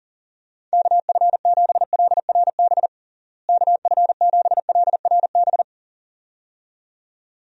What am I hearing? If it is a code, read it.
Morse code, heard as a keyed tone, about 40 wpm, 700 Hz. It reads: KF7LRB KF7LRB
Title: USA Call Signs - 6 Characters - Repeated 2x - 40wpm